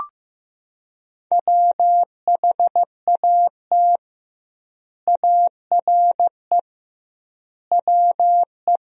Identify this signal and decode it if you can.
Morse code, heard as a keyed tone, about 15 wpm, 700 Hz. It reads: WHAT ARE WE
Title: 2x Word-Spacing - Sentences from Top 100 Words - 15wpm